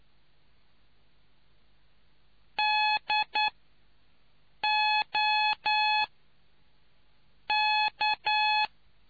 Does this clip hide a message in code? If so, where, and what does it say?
Yes: Morse code from 2.6 to 8.7 s, heard as a keyed tone, about 9 wpm, 850 Hz. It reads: D O K